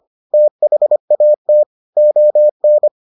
Morse code, heard as a keyed tone, about 25 wpm, 600 Hz. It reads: THAT ON